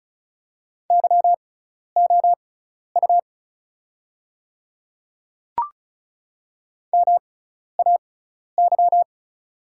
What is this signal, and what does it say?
Morse code, heard as a keyed tone, about 36 wpm, spaced as for 12 wpm, 700 Hz. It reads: YOU MAY
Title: 6x Character Spacing - Top 500 Words - 35wpm